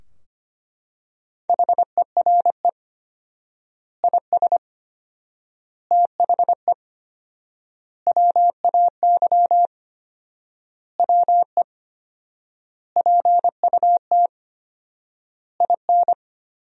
Morse code, heard as a keyed tone, about 25 wpm, 700 Hz. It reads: HERE IS THE WAY WE PUT IN
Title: 4x Word-Spacing - Sentences from Top 200 Words - 25wpm